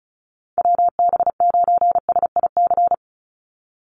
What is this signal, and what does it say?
Morse code, heard as a keyed tone, about 34 wpm, 700 Hz. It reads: WB9SIC
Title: USA Call Signs 35wpm